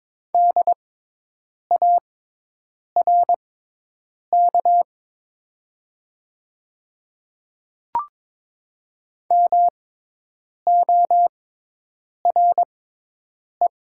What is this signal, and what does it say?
Morse code, heard as a keyed tone, about 22 wpm, spaced as for 8 wpm, 700 Hz. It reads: DARK MORE
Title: 6x Character Spacing - Top 500 Words - 22wpm